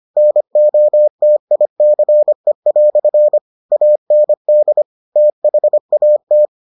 Morse code, heard as a keyed tone, about 25 wpm, 600 Hz. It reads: NOTICE" AND THAT